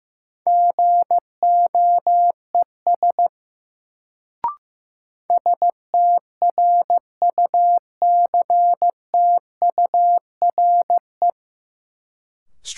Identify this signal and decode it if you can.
Morse code, heard as a keyed tone, about 15 wpm, 700 Hz. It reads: GOES STRUCTURE
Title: Sentences from Top 1500 Words - New Word Drill Alt - 15wpm